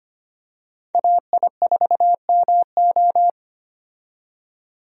Morse code, heard as a keyed tone, about 25 wpm, 700 Hz. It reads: AI4MO
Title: USA Call Signs - 5 Characters - 25wpm